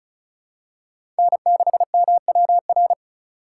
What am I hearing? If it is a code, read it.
Morse code, heard as a keyed tone, about 36 wpm, 700 Hz. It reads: N6MWR
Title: USA Call Signs - 5 Characters - 4x Character Spacing - 35wpm